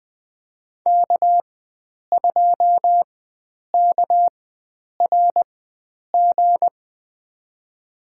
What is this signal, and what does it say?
Morse code, heard as a keyed tone, about 20 wpm, spaced as for 9 wpm, 700 Hz. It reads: K2KRG
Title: USA Call Signs - 5 Characters - 4x Character Spacing - 20wpm